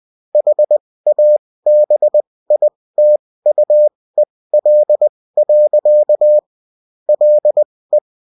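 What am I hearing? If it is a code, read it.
Morse code, heard as a keyed tone, about 20 wpm, spaced as for 16 wpm, 600 Hz. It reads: HABITUEL. LE
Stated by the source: Bulletin in text and CW form.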